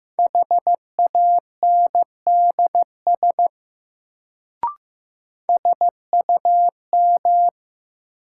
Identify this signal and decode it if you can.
Morse code, heard as a keyed tone, about 15 wpm, 700 Hz. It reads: HANDS SUM
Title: Sentences from Top 1000 Words - Review Part 6 - New Word Drill - 15wpm